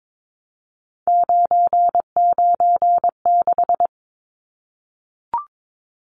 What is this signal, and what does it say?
Morse code, heard as a keyed tone, about 22 wpm, 700 Hz. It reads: 996
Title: Numbers - 3-Digit 22wpm